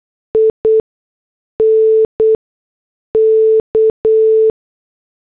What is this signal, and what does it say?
Morse code, heard as a keyed tone, about 8 wpm, spaced as for 6 wpm, 440 Hz. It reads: INK